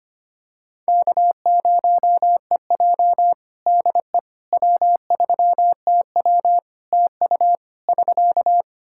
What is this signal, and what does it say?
Morse code, heard as a keyed tone, about 25 wpm, 700 Hz. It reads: K0EJ DE W3TW TU <SK>
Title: Practice QSO - Exchange 4 - Ending - 25wpm